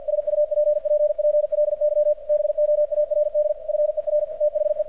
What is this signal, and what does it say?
Morse code, heard as a keyed tone, about 46 wpm, 600 Hz, with radio noise. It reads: SUPPPLY BOARD FER THE
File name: archived / CRSnet